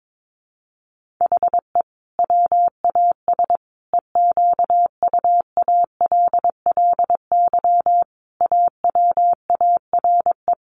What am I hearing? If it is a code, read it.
Morse code, heard as a keyed tone, about 22 wpm, 700 Hz. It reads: HE WAS EQUALLY AWARE
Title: Sentences from Top 1400 Words 22wpm